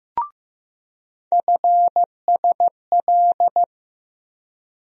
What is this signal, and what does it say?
Morse code, heard as a keyed tone, about 15 wpm, 700 Hz. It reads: FSL
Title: USA Call Sign Postfixes - 3 Letters 15wpm